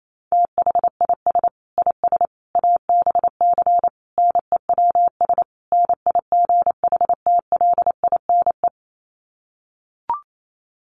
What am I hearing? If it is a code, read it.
Morse code, heard as a keyed tone, about 28 wpm, 700 Hz. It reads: THIS IS ABC NEWS NIGHTLINE
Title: Sets of 5 Words 28wpm